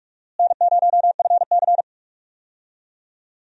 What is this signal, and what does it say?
Morse code, heard as a keyed tone, about 46 wpm, 700 Hz. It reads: N0FC
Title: USA Call Signs - 4 Characters - Repeated 2x - 45wpm